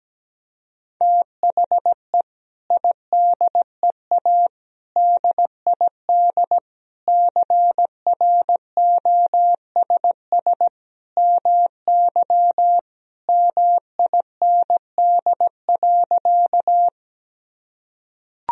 Morse code, heard as a keyed tone, about 17 wpm, 700 Hz. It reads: THE IDEA DID CROSS MY MIND.
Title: Sentences from Top 1000 Words - Review Part 1 - 17wpm